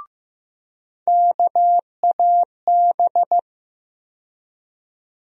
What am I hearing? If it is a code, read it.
Morse code, heard as a keyed tone, about 15 wpm, 700 Hz. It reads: KAB
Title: USA Call Sign Postfixes - 3 Letters 15wpm